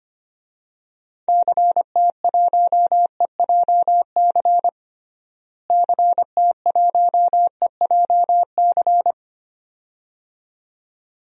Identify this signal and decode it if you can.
Morse code, heard as a keyed tone, about 25 wpm, 700 Hz. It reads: CT1EJC CT1EJC
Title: DXCC-100 Call Signs - 6 Characters - Repeated 2x - 25wpm